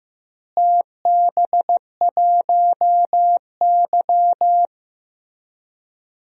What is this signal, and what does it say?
Morse code, heard as a keyed tone, about 15 wpm, 700 Hz. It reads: TB1Y
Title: DXCC-100 Call Signs - 4 Characters - Repeated 3x - 15wpm